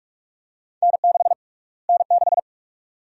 Morse code, heard as a keyed tone, about 46 wpm, 700 Hz. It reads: N6 N6
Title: USA Call Sign Prefixes - Repeated 2x - 45wpm